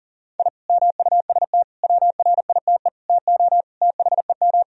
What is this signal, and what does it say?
Morse code, heard as a keyed tone, about 40 wpm, 700 Hz. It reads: I MUST WRITE TO THEM
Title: Sentences from Top 1000 Words - Review Part 1 - 40wpm